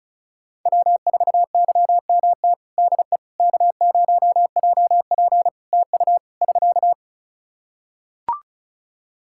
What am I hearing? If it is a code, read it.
Morse code, heard as a keyed tone, about 36 wpm, 700 Hz. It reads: W4YMT DE K0JP TU <SK>
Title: Practice QSO - Exchange 4 - Ending - 35wpm